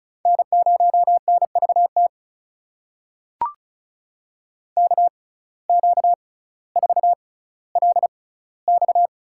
Morse code, heard as a keyed tone, about 34 wpm, spaced as for 18 wpm, 700 Hz. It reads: N0NVT KQ4LX
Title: USA Call Signs - 5 Characters - 6x Character Spacing - 35wpm